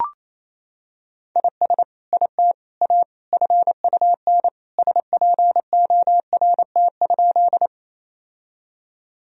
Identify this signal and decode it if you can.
Morse code, heard as a keyed tone, about 28 wpm, 700 Hz. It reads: IS IT A FUN SPORT?
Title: Sentences from Top 1000 Words - Review Part 4 - 28wpm